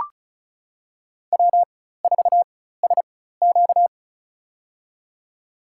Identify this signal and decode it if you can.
Morse code, heard as a keyed tone, about 36 wpm, spaced as for 16 wpm, 700 Hz. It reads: W4SQ
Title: USA Call Signs - 4x Character Spacing - 35wpm